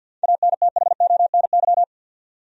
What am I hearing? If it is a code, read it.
Morse code, heard as a keyed tone, about 50 wpm, 700 Hz. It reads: ANTHONY
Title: USA Common Names - 50wpm